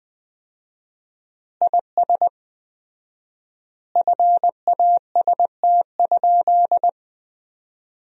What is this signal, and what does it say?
Morse code, heard as a keyed tone, about 19 wpm, 700 Hz. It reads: IS FAST?
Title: 4x Word-Spacing - Sentences from Top 400 Words - 20wpm